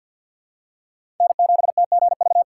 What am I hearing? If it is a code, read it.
Morse code, heard as a keyed tone, about 53 wpm, 700 Hz. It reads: N7TGV
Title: USA Call Signs - 4x Character Spacing - 50wpm